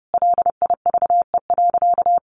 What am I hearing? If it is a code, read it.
Morse code, heard as a keyed tone, about 30 wpm, 700 Hz. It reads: LIVE.
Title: Sentences from Top 1100 Words 30wpm